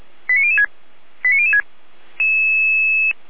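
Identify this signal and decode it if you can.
Five-tone selective call (ZVEI), heard as quick runs of five tones, tones of about 70 ms. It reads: 89006 89006
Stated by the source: ZVEI five-tone call 89150 received by SDR.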